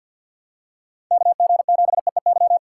Morse code, heard as a keyed tone, about 50 wpm, 700 Hz. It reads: KG7EEY
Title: USA Call Signs - 6x Character Spacing - 50wpm